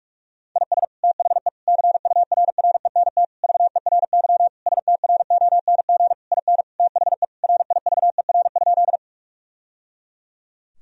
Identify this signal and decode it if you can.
Morse code, heard as a keyed tone, about 42 wpm, 700 Hz. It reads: IS THE CURRENT VERY STRONG IN THE RIVER?
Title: Sentences from Top 1000 Words - Review Part 4 - 45wpm